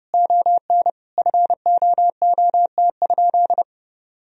Morse code, heard as a keyed tone, about 30 wpm, 700 Hz. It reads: ON FOOT?
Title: Sentences from Top 1000 Words - Review Part 3 - 30wpm